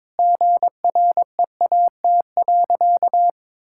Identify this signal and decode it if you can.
Morse code, heard as a keyed tone, about 22 wpm, 700 Hz. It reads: GREAT.
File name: practice / 22wpm / Sentences with 2-Word Phrases - 22wpm